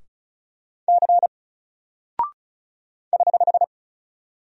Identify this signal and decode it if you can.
Morse code, heard as a keyed tone, about 34 wpm, 700 Hz. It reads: C <HH>
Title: Single Character (Letter:Number:Punctuation:Prosign) Alt 35wpm